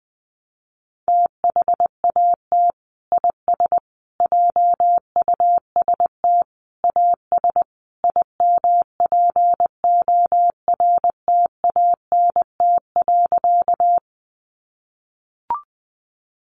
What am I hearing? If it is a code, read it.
Morse code, heard as a keyed tone, about 20 wpm, 700 Hz. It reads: THAT IS JUST AS IMPORTANT.
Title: Sentences from Top 600 Words 20wpm